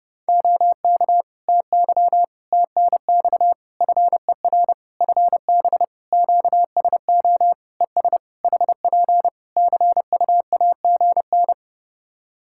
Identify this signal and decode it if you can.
Morse code, heard as a keyed tone, about 32 wpm, 700 Hz. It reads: OK TY TNX FER FB QSO ES HP CUAGN
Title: K1USN SST QSO Practice 30wpm